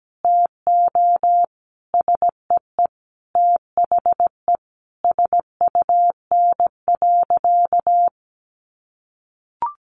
Morse code, heard as a keyed tone, about 17 wpm, 700 Hz. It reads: TO SEE THE SUN.
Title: Easy Phrases 17wpm